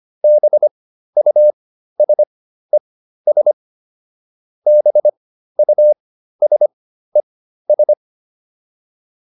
Morse code, heard as a keyed tone, about 25 wpm, spaced as for 13 wpm, 600 Hz. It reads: BUSES BUSES